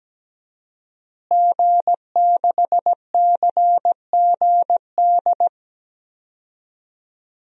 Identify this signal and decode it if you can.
Morse code, heard as a keyed tone, about 17 wpm, 700 Hz. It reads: G6CGD
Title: DXCC-100 Call Signs - 5 Characters - Repeated 2x - 17wpm